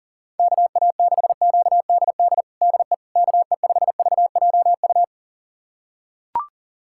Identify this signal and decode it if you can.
Morse code, heard as a keyed tone, about 40 wpm, 700 Hz. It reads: KA6QDD DE KE5VJU
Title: USA Callsign DE Callsign 40wpm